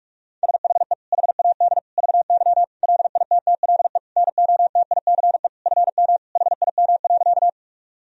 Morse code, heard as a keyed tone, about 46 wpm, 700 Hz. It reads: SHE HAD VY LITTLE NOTICE FM HIM.
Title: Sentences from Top 400 Words and Abbreviations 45wpm